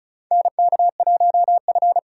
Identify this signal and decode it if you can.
Morse code, heard as a keyed tone, about 36 wpm, 700 Hz. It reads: NK1F
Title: USA Call Signs - 4 Characters - 4x Character Spacing - 35wpm